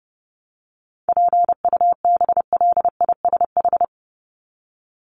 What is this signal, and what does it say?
Morse code, heard as a keyed tone, about 30 wpm, 700 Hz. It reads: PUBLISH
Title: ED Suffix - Top 100 Words with Base Words 30wpm